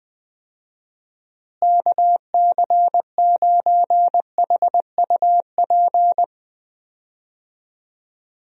Morse code, heard as a keyed tone, about 20 wpm, 700 Hz. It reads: KC9HUP
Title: USA Call Signs - 6 Characters - Repeated 3x - 20wpm